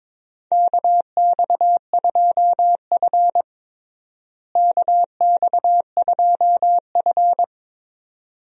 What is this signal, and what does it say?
Morse code, heard as a keyed tone, about 22 wpm, 700 Hz. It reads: KX2F KX2F
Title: USA Call Signs - 4 Characters - Repeated 2x - 22wpm